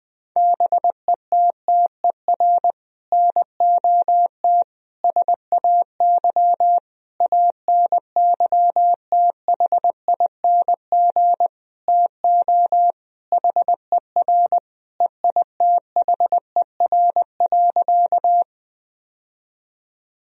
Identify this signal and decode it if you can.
Morse code, heard as a keyed tone, about 20 wpm, 700 Hz. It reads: BETTER NOT SAY ANYTHING TO HER EITHER.
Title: Sentences from Top 1000 Words - Review Part 4 - 20wpm